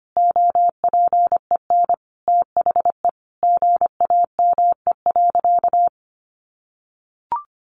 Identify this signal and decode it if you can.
Morse code, heard as a keyed tone, about 25 wpm, 700 Hz. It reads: OPEN THE GAME.
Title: Sentences from Top 1500 Words 25wpm